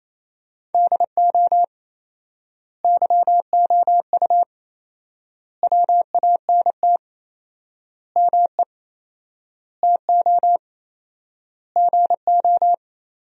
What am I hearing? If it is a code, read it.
Morse code, heard as a keyed tone, about 28 wpm, 700 Hz. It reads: DO YOU WANT ME TO GO
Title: 4x Word-Spacing - Sentences from Top 200 Words - 28wpm